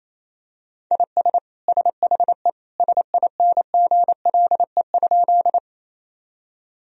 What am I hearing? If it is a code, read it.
Morse code, heard as a keyed tone, about 28 wpm, 700 Hz. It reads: IS SHE SINGLE?
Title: Sentences from Top 1000 Words - Review Part 4 - 28wpm